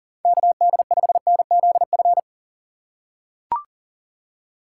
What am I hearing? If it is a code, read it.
Morse code, heard as a keyed tone, about 40 wpm, 700 Hz. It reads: KD5NZF
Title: USA Call Signs - 6 Characters - 6x Character Spacing - 40wpm